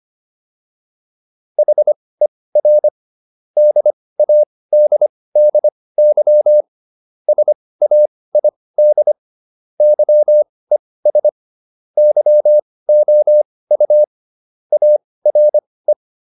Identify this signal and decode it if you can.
Morse code, heard as a keyed tone, about 25 wpm, spaced as for 18 wpm, 600 Hz. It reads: HER DADDY SAID YES YOU ARE
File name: SS123 18